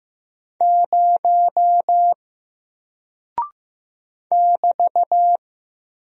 Morse code, heard as a keyed tone, about 15 wpm, 700 Hz. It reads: 0 <BT>
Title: Single Character (Letter:Number:Punctuation:Prosign) 15wpm